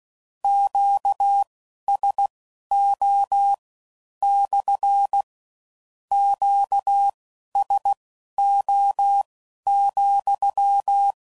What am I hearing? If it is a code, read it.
Morse code, heard as a keyed tone, about 16 wpm, spaced as for 11 wpm, 800 Hz. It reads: QSO / QSO,
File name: CW Code Groups @15WPM / Q Calls / Q Calls SPSL2